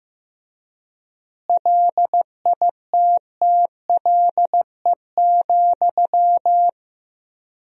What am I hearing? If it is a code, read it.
Morse code, heard as a keyed tone, about 15 wpm, 700 Hz. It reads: LITTLE,
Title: 4x Word-Spacing - Sentences from Top 400 Words - 15wpm